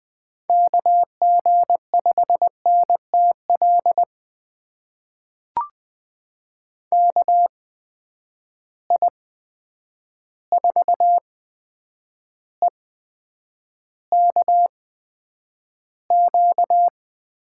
Morse code, heard as a keyed tone, about 20 wpm, spaced as for 9 wpm, 700 Hz. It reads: KG5NTL KI4EKQ
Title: USA Call Signs - 6 Characters - 8x Character Spacing - 20wpm